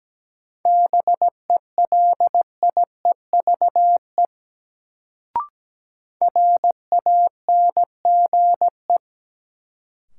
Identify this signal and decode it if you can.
Morse code, heard as a keyed tone, about 17 wpm, 700 Hz. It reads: BELIEVE RANGE
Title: Sets of 4 Words - New Word Drill - 17wpm